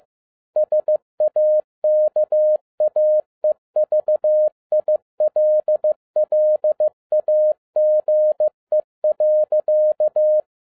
Morse code, heard as a keyed tone, about 15 wpm, 610 Hz. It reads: SAKAE VILLAGE.
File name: 2012-01-mp3 / n13-A NEWLY DESIGNED QSL CARD-15wpm